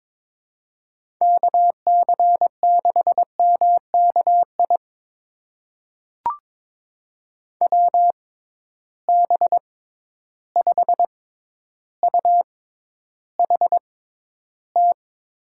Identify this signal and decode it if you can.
Morse code, heard as a keyed tone, about 22 wpm, spaced as for 11 wpm, 700 Hz. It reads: KC6MKI WB5UHT
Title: USA Call Signs - 6 Characters - 6x Character Spacing - 22wpm